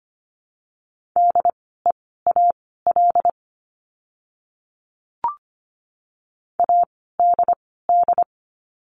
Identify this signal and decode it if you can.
Morse code, heard as a keyed tone, about 25 wpm, spaced as for 16 wpm, 700 Hz. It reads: DEAL ADD
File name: course / 25:10wpm / 094 Lesson 16f - Letter c - Words 25wpm